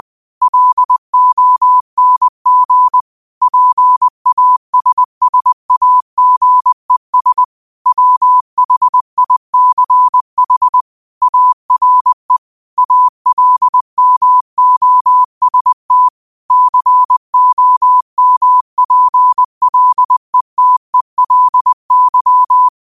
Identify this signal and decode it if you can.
Morse code, heard as a keyed tone, about 20 wpm, 1000 Hz. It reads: LONG PASSAGES WHICH ARE ALMOST COMPLETELY
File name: qotdfiles / 20wpm